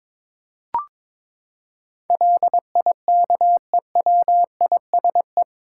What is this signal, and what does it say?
Morse code, heard as a keyed tone, about 22 wpm, 700 Hz. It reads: LIKEWISE
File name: practice / 22wpm / Eight Letter Words - 22wpm